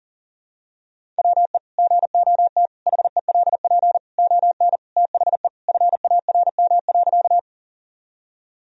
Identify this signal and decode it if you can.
Morse code, heard as a keyed tone, about 40 wpm, 700 Hz. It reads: WE GOT HELP ON THE FARM.
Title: Sentences from Top 1000 Words - Review Part 2 - 40wpm